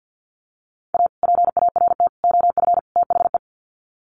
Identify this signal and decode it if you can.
Morse code, heard as a keyed tone, about 50 wpm, 700 Hz. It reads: A PART OF THE
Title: Sets of 4 Words 50wpm